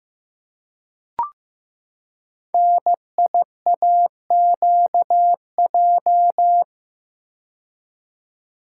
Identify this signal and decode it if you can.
Morse code, heard as a keyed tone, about 15 wpm, 700 Hz. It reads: NIAQJ
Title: Code Groups - 5 Letters - 15wpm